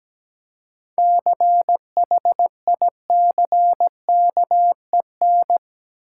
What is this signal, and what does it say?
Morse code, heard as a keyed tone, about 17 wpm, 700 Hz. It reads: CHICKEN
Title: Seven Letter Words - 17wpm